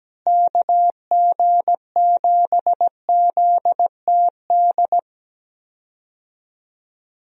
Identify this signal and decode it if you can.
Morse code, heard as a keyed tone, about 17 wpm, 700 Hz. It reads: KG7ZTD
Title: USA Call Signs - 6 Characters - Repeated 3x - 17wpm